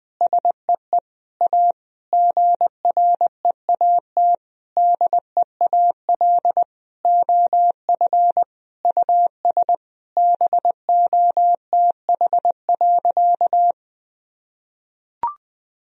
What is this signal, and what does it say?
Morse code, heard as a keyed tone, about 20 wpm, 700 Hz. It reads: SEE A GREAT DEAL OF US BOTH.
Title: Sentences from Top 1000 Words - Review Part 4 - 20wpm